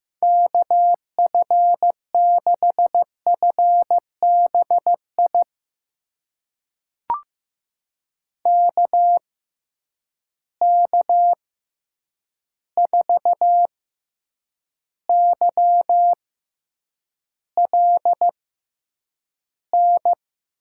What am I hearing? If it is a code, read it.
Morse code, heard as a keyed tone, about 15 wpm, spaced as for 8 wpm, 700 Hz. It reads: KF6FBI KK4YLN
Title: USA Call Signs - 6 Characters - 6x Character Spacing - 15wpm